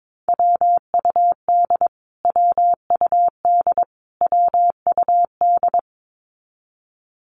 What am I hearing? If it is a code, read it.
Morse code, heard as a keyed tone, about 22 wpm, 700 Hz. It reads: WUD WUD WUD
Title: Abbreviations Repeated 3x - 22wpm